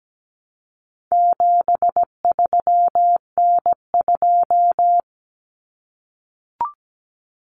Morse code, heard as a keyed tone, about 17 wpm, 700 Hz. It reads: 73N2
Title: Common Cut Numbers - 4-Digit 17wpm